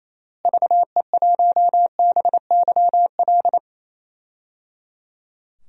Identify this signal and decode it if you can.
Morse code, heard as a keyed tone, about 27 wpm, 700 Hz. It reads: VE1BYL
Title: DXCC-100 Call Signs - 6 Characters - Repeated 3x - 28wpm